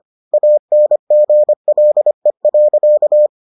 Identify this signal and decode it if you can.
Morse code, heard as a keyed tone, about 25 wpm, 600 Hz. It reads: ANGLE.